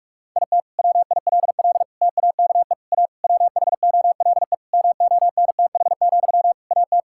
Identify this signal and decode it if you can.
Morse code, heard as a keyed tone, about 46 wpm, 700 Hz. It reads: IT WILL TAKE A WHOLE MONTH, AT
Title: Sentences from Top 1000 Words - Review Part 5 - 45wpm